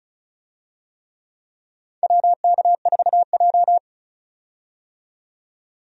Morse code, heard as a keyed tone, about 36 wpm, 700 Hz. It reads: WK4J
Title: USA Call Signs - 4 Characters - 35wpm